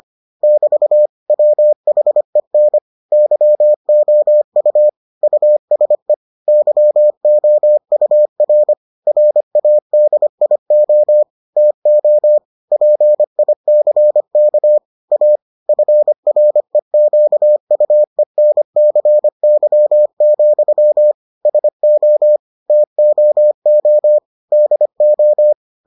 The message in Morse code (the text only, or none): <BT> WHEN YOU USE YOUR RADIO TO PICK A FREQUENCY, SO TOO DO